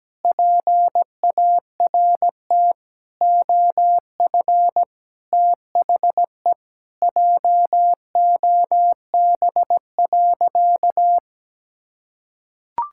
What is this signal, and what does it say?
Morse code, heard as a keyed tone, about 17 wpm, 700 Hz. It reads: PART OF THE JOB.
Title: Sentences from Top 1000 Words - Review Part 5 - 17wpm